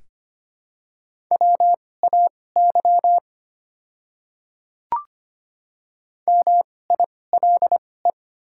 Morse code, heard as a keyed tone, about 25 wpm, spaced as for 18 wpm, 700 Hz. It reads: WAY MILE
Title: 2x Character Spacing - Top 300 Words - 25wpm